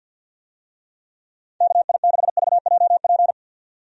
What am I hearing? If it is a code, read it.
Morse code, heard as a keyed tone, about 52 wpm, 700 Hz. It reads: KI6VJP
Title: USA Call Signs - 6 Characters - 50wpm